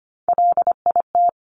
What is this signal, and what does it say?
Morse code, heard as a keyed tone, about 25 wpm, 700 Hz. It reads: LIT
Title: Top Three Letter Combinations in Words 25wpm